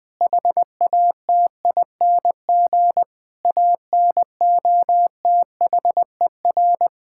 Morse code, heard as a keyed tone, about 20 wpm, 700 Hz. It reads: HATING ANOTHER
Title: Sets of 9 Words - 20wpm